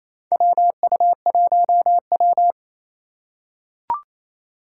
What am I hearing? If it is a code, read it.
Morse code, heard as a keyed tone, about 27 wpm, 700 Hz. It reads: WU1W
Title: USA Call Signs - 4 Characters - 3x Character Spacing - 28wpm